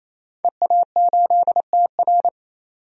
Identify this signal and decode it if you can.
Morse code, heard as a keyed tone, about 28 wpm, 700 Hz. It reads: EA8TR
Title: DXCC-100 Call Signs - 5 Characters - 28wpm